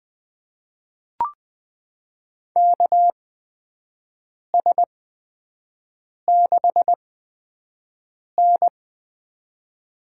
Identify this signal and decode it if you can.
Morse code, heard as a keyed tone, about 21 wpm, spaced as for 5 wpm, 700 Hz. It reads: KS6N
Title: USA Call Signs - 4 Characters - 8x Character Spacing - 20wpm